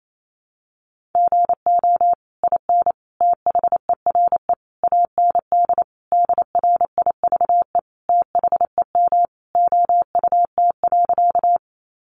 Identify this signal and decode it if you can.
Morse code, heard as a keyed tone, about 28 wpm, 700 Hz. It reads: GO IN THERE AND DRIVE THEM OUT.
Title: Sentences from Top 500 Words 28wpm